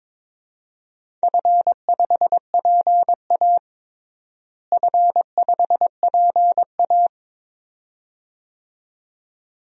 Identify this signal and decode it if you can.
Morse code, heard as a keyed tone, about 21 wpm, 700 Hz. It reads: F5PA F5PA
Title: DXCC-100 Call Signs - 4 Characters - Repeated 2x - 22wpm